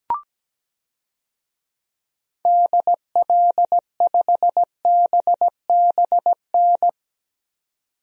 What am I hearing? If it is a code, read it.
Morse code, heard as a keyed tone, about 17 wpm, 700 Hz. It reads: DL5BBN
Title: DXCC-100 Call Signs - 6 Characters - 17wpm